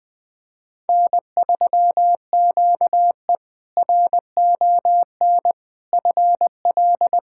Morse code, heard as a keyed tone, about 20 wpm, 700 Hz. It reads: N3QE RON FL
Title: K1USN SST QSO Practice - 20wpm